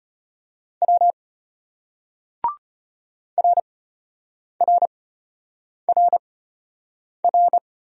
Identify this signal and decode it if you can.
Morse code, heard as a keyed tone, about 33 wpm, spaced as for 8 wpm, 700 Hz. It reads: W RRRR